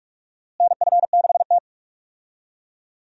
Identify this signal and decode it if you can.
Morse code, heard as a keyed tone, about 46 wpm, 700 Hz. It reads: NF6T
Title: USA Call Signs - 4 Characters - Repeated 2x - 45wpm